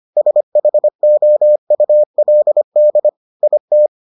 Morse code, heard as a keyed tone, about 25 wpm, 600 Hz. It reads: SHOULD IT